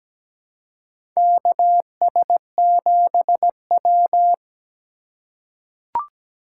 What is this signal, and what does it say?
Morse code, heard as a keyed tone, about 17 wpm, 700 Hz. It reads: KS7W
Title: USA Call Signs - 4 Characters - 4x Character Spacing - 17wpm